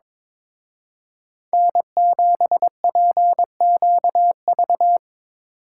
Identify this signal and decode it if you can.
Morse code, heard as a keyed tone, about 25 wpm, 700 Hz. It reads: N7PQV
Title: Speed Racing - USA Call Signs 22wpm